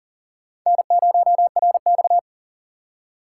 Morse code, heard as a keyed tone, about 53 wpm, spaced as for 47 wpm, 700 Hz. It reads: N0RX
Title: Speed Racing - USA 1x2 Call Signs 40wpm